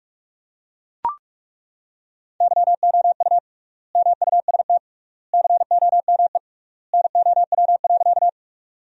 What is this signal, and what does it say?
Morse code, heard as a keyed tone, about 42 wpm, 700 Hz. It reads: YOU MUST COME NOW.
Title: 3x Word-Spacing - Sentences from Top 200 Words - 45wpm